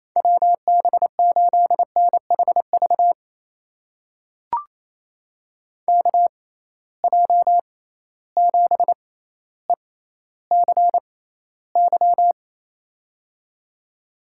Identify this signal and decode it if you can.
Morse code, heard as a keyed tone, about 28 wpm, spaced as for 14 wpm, 700 Hz. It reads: WB8NHV KJ7ECY
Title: USA Call Signs - 6 Characters - 6x Character Spacing - 28wpm